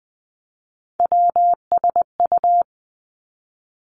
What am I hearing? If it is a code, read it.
Morse code, heard as a keyed tone, about 20 wpm, 700 Hz. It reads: WSU
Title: Three Challenging Letter-Number Combinations 20wpm